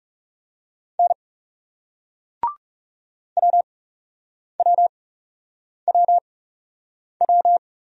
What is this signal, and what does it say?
Morse code, heard as a keyed tone, about 41 wpm, spaced as for 8 wpm, 700 Hz. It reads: N WWWW